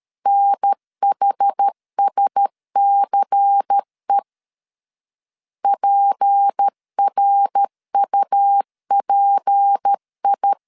Morse code, heard as a keyed tone, about 13 wpm, 800 Hz. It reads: NHSCE PRUPI
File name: LEZIONI / 6 PKX0 / 6 P 12wpm 2033